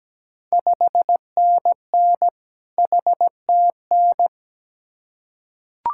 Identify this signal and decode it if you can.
Morse code, heard as a keyed tone, about 17 wpm, 690 Hz. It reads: 5NN HTN
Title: Ontario QSO Party Exchange 17wpm